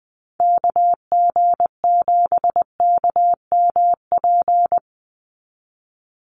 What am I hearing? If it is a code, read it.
Morse code, heard as a keyed tone, about 20 wpm, 700 Hz. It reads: KG7KMP
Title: USA Call Signs 20wpm